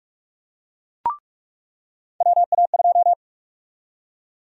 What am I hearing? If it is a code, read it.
Morse code, heard as a keyed tone, about 47 wpm, 700 Hz. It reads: WA2
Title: USA Call Sign Prefixes - 45wpm